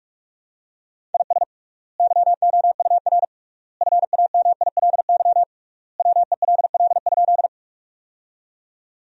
Morse code, heard as a keyed tone, about 42 wpm, 700 Hz. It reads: IS YOUR FAMILY WELL?
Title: 3x Word-Spacing - Sentences from Top 400 Words - 45wpm